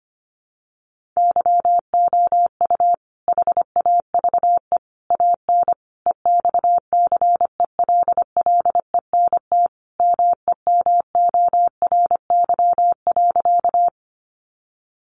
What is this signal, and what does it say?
Morse code, heard as a keyed tone, about 25 wpm, 700 Hz. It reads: YOU HAVE AN EXCELLENT MEMORY.
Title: Sentences from Top 1300 Words 25wpm